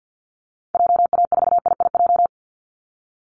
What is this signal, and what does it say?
Morse code, heard as a keyed tone, about 50 wpm, 700 Hz. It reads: JA4IIJ
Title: Call Signs - 50wpm